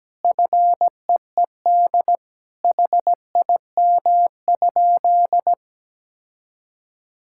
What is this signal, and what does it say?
Morse code, heard as a keyed tone, about 17 wpm, 700 Hz. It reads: FEED HIM?
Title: Sentences from Top 1600 Words 17wpm